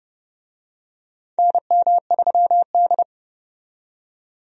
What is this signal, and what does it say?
Morse code, heard as a keyed tone, about 30 wpm, 700 Hz. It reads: NM3D
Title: USA Call Signs - 4 Characters - 30wpm